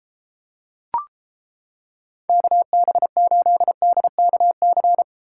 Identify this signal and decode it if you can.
Morse code, heard as a keyed tone, about 29 wpm, 700 Hz. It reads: KB8DKC